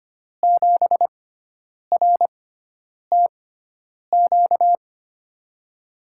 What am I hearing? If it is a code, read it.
Morse code, heard as a keyed tone, about 25 wpm, spaced as for 9 wpm, 700 Hz. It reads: 7RTQ
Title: USA Call Signs - 6x Character Spacing - 25wpm